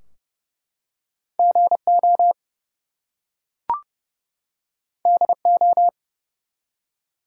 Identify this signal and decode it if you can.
Morse code, heard as a keyed tone, about 29 wpm, 700 Hz. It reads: GO DO
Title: Two Letter Words - 30wpm